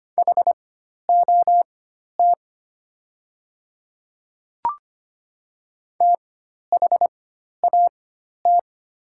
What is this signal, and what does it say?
Morse code, heard as a keyed tone, about 25 wpm, spaced as for 12 wpm, 700 Hz. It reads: HOT THAT
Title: 4x Character Spacing - Top 100 Words - 25wpm